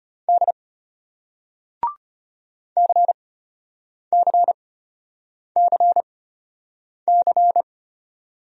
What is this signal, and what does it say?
Morse code, heard as a keyed tone, about 35 wpm, spaced as for 8 wpm, 700 Hz. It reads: D CCCC